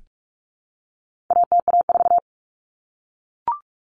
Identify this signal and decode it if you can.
Morse code, heard as a keyed tone, about 44 wpm, 700 Hz. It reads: ATA4
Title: Common Cut Numbers - 4-Digit 45wpm